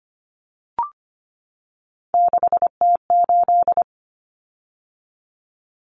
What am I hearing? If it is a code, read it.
Morse code, heard as a keyed tone, about 25 wpm, 700 Hz. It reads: 6T8
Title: Common Cut Numbers - 3-Digit 25wpm